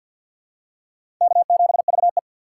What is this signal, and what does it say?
Morse code, heard as a keyed tone, about 46 wpm, 700 Hz. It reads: K7VE
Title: USA Call Signs - 4 Characters - 2x Character Spacing - 50wpm